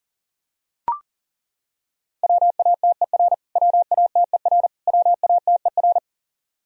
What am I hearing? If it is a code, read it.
Morse code, heard as a keyed tone, about 40 wpm, 700 Hz. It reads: WATER WATER WATER
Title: Top 100 Words Repeated 40wpm